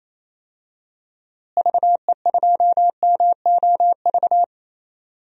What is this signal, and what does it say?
Morse code, heard as a keyed tone, about 28 wpm, 700 Hz. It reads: VE2MOV
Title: DXCC-100 Call Signs - 6 Characters - Repeated 3x - 28wpm